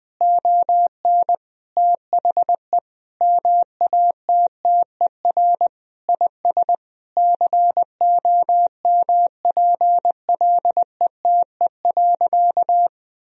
Morse code, heard as a keyed tone, about 20 wpm, 700 Hz. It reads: ON THE MATTER IS COMPLETE.
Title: Sentences from Top 1000 Words - Review Part 2 - 20wpm